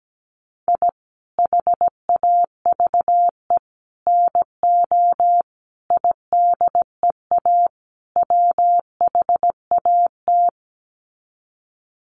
Morse code, heard as a keyed tone, about 17 wpm, 700 Hz. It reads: I HAVE NO IDEA WHAT
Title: Sets of 5 Words Encore 17wpm